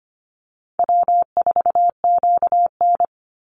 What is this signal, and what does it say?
Morse code, heard as a keyed tone, about 25 wpm, 700 Hz. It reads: W4QN
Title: Easy Call Signs Repeated 3x - 25wpm